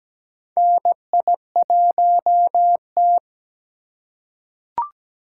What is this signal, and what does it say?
Morse code, heard as a keyed tone, about 17 wpm, 700 Hz. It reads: NI1T
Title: USA Call Signs - 4x Character Spacing - 17wpm